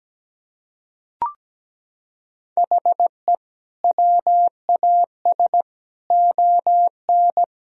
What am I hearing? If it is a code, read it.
Morse code, heard as a keyed tone, about 17 wpm, 700 Hz. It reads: HE WAS ON
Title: Sentences from Top 1000 Words - Review Part 3 - 17wpm